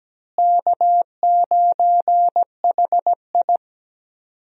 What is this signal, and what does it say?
Morse code, heard as a keyed tone, about 17 wpm, 700 Hz. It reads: K9HI
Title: USA Call Signs - 4 Characters - Repeated 2x - 17wpm